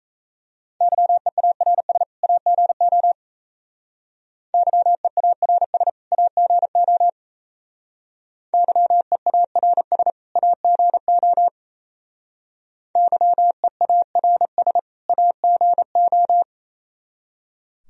Morse code, heard as a keyed tone, about 34 wpm, 700 Hz. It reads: YEARS AGO YEARS AGO YEARS AGO YEARS AGO
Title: Speed-Racing - Sets of 2 Words 28wpm